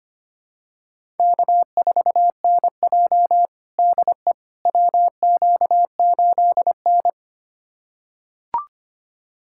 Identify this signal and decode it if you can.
Morse code, heard as a keyed tone, about 25 wpm, 700 Hz. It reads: K4NJ DE WQ8N
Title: USA Callsign DE Callsign - 4 Characters - 25wpm